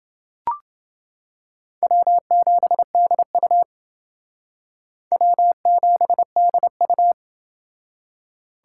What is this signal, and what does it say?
Morse code, heard as a keyed tone, about 28 wpm, 700 Hz. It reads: W7DU W7DU